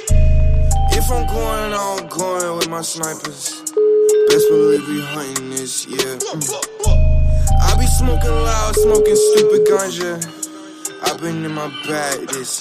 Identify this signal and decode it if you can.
Ringback tone (Europe, 425 Hz), with speech.